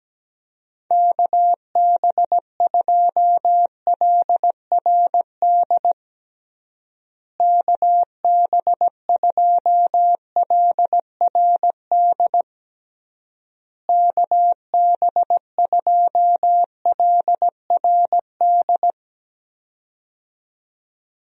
Morse code, heard as a keyed tone, about 17 wpm, 700 Hz. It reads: KB2LRD KB2LRD KB2LRD
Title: USA Call Signs - 6 Characters - Repeated 3x - 17wpm